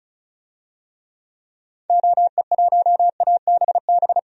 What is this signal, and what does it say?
Morse code, heard as a keyed tone, about 36 wpm, 700 Hz. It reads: OE1ABB
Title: DXCC-100 Call Signs - 6 Characters - 35wpm